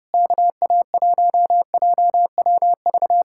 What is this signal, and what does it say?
Morse code, heard as a keyed tone, about 37 wpm, 700 Hz. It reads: KA1JWV
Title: Speed-Racing Common QSO Phrases - 30wpm